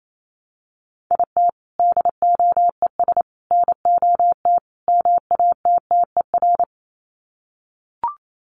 Morse code, heard as a keyed tone, about 28 wpm, 700 Hz. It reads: IT DOES NOT MATTER
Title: Sets of 4 Words 28wpm